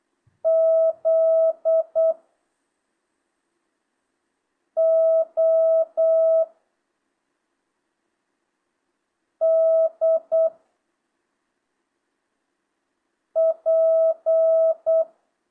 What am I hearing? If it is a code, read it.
Morse code, heard as a keyed tone, about 8 wpm, 650 Hz. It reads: Z O D P
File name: jh8cbh / 2025cw